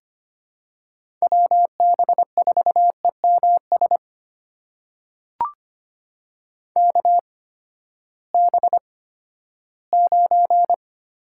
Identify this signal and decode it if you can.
Morse code, heard as a keyed tone, about 25 wpm, spaced as for 14 wpm, 700 Hz. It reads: WB4EMS KB9
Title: USA Call Signs - 6 Characters - 8x Character Spacing - 25wpm